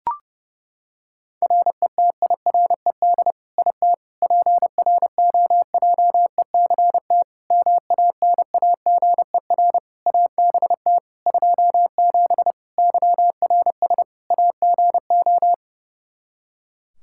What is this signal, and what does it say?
Morse code, heard as a keyed tone, about 32 wpm, 700 Hz. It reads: RETIRED IT PROJECT MANAGER ABT 27 YRS AGO
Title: Practice QSO - Exchange 3 - More Info - 30wpm